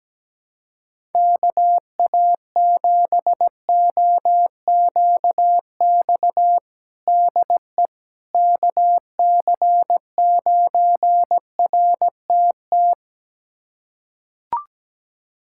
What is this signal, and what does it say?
Morse code, heard as a keyed tone, about 17 wpm, 700 Hz. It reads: KA7OQX DE KC9RTT
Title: USA Callsign DE Callsign - 6 Characters - 17wpm